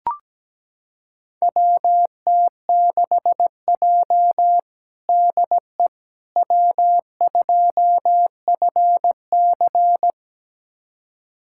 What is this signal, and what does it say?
Morse code, heard as a keyed tone, about 17 wpm, 700 Hz. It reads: WT6J DE W2FC
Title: USA Callsign DE Callsign - 4 Characters - 17wpm